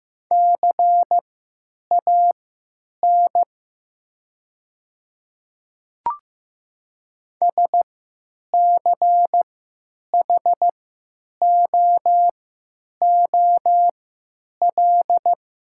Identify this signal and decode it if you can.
Morse code, heard as a keyed tone, about 16 wpm, spaced as for 9 wpm, 700 Hz. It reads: CAN SCHOOL
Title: 3x Character Spacing - Top 500 Words - 15wpm